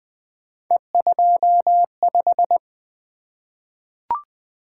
Morse code, heard as a keyed tone, about 20 wpm, 700 Hz. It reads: E25
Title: Three Letter-Number Combinations - 20wpm